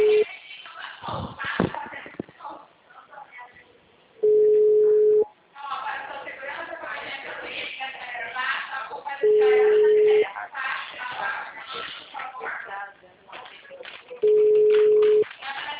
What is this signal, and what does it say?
Ringback tone (Europe, 425 Hz), with speech.
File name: out- -2001- - - .44183